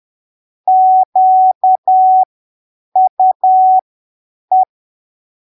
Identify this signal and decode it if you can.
Morse code, heard as a keyed tone, about 10 wpm, spaced as for 7 wpm, 750 Hz. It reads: QUE
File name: arquivos / texto / texto 26 10ppm